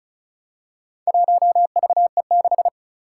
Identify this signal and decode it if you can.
Morse code, heard as a keyed tone, about 35 wpm, 700 Hz. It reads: 1VE6
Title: Four Challenging Letter-Number Combinations 35wpm